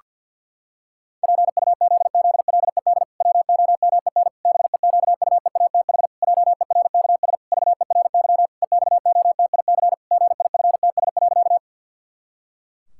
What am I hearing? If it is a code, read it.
Morse code, heard as a keyed tone, about 50 wpm, 700 Hz. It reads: PUZZLED WOMEN BEQUEATH JERKS VERY EXOTIC GIFTS.
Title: Pangram Challenge v2 - 50wpm